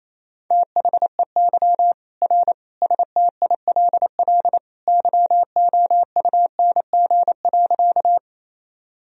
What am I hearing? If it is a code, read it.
Morse code, heard as a keyed tone, about 28 wpm, 700 Hz. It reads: THEY R STILL YOUNG.
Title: Sentences from Top 400 Words and Abbreviations 28wpm